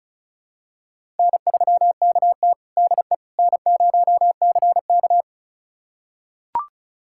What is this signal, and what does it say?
Morse code, heard as a keyed tone, about 34 wpm, 700 Hz. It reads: N3KT DE N0CK
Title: USA Callsign DE Callsign - 4 Characters - 35wpm